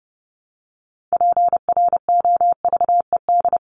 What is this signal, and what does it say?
Morse code, heard as a keyed tone, about 30 wpm, 700 Hz. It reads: PROVED
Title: ED Suffix - Top 100 Words with Base Words 30wpm